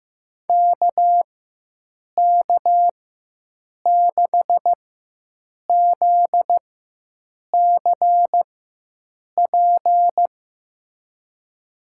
Morse code, heard as a keyed tone, about 15 wpm, spaced as for 7 wpm, 700 Hz. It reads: KK6ZCP
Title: USA Call Signs - 6 Characters - 4x Character Spacing - 15wpm